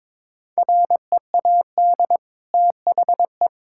Morse code, heard as a keyed tone, about 22 wpm, 700 Hz. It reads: READ THE
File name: practice / 22wpm / Easy Sentences with Hints - 22wpm